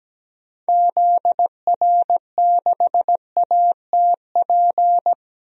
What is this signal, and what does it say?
Morse code, heard as a keyed tone, about 17 wpm, 700 Hz. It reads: ZR6ATP
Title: DXCC-100 Call Signs - 6 Characters - 17wpm